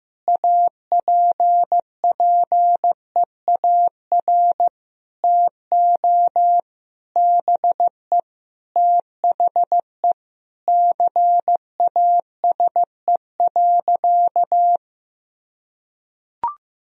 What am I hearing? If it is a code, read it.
Morse code, heard as a keyed tone, about 16 wpm, 700 Hz. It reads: APPEAR TO BE THE CASE.
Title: Sentences from Top 1000 Words - Review Part 2 - 15wpm